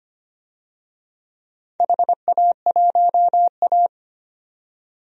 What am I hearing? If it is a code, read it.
Morse code, heard as a keyed tone, about 25 wpm, 700 Hz. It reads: HA1A
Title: DXCC-100 Call Signs - 4 Characters - 25wpm